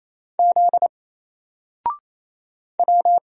Morse code, heard as a keyed tone, about 27 wpm, 700 Hz. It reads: Z W
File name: Single-Letter-Number-28wpm